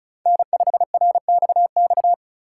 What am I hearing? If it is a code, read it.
Morse code, heard as a keyed tone, about 36 wpm, 700 Hz. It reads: N5RXX
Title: USA Call Signs - 2x Character Spacing - 35wpm